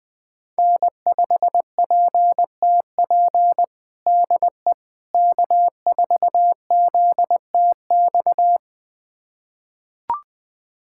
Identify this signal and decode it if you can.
Morse code, heard as a keyed tone, about 20 wpm, 700 Hz. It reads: N5PTP DE K4ZTX
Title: USA Callsign DE Callsign - 5 Characters - 20wpm